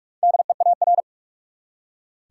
Morse code, heard as a keyed tone, about 46 wpm, 700 Hz. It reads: DEAR
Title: Four Letter Words - 45wpm